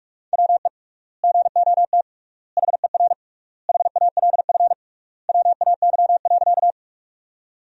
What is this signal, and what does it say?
Morse code, heard as a keyed tone, about 43 wpm, 700 Hz. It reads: WE GOT HER HALF WAY.
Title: 3x Word-Spacing - Sentences from Top 400 Words - 45wpm